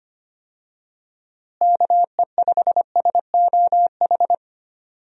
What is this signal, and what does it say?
Morse code, heard as a keyed tone, about 25 wpm, 700 Hz. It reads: KE5SOH
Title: USA Call Signs - 6 Characters - Repeated 3x - 25wpm